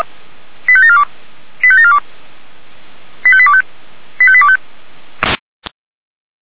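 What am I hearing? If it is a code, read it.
Five-tone selective call (ZVEI), heard as quick runs of five tones, tones of about 70 ms. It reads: 85742 85742 85825 85825
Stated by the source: ZVEI five-tone selective call received with an SDR.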